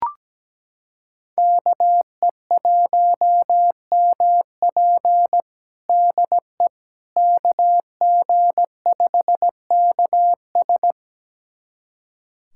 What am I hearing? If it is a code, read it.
Morse code, heard as a keyed tone, about 17 wpm, 700 Hz. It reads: KE1MP DE KG5KS
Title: USA Callsign DE Callsign - 5 Characters - 17wpm